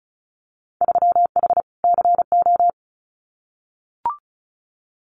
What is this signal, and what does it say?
Morse code, heard as a keyed tone, about 36 wpm, 700 Hz. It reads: 3H CO
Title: Winter Field Day Exchanges 35wpm